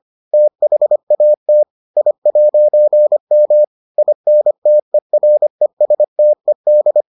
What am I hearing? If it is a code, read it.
Morse code, heard as a keyed tone, about 25 wpm, 600 Hz. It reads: THAT I'M INTERESTED